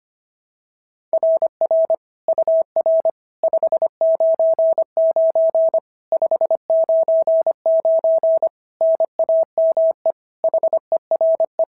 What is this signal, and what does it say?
Morse code, heard as a keyed tone, about 22 wpm, 650 Hz. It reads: RR UR 599 599 NAME HERE
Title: SOTA QSO Practice 25wpm